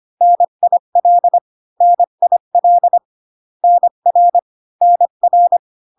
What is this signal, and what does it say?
Morse code, heard as a keyed tone, about 25 wpm, 700 Hz. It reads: NIL NIL NR NR